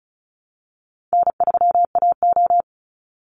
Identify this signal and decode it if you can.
Morse code, heard as a keyed tone, about 34 wpm, 700 Hz. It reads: N3AO
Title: Call Signs - 35wpm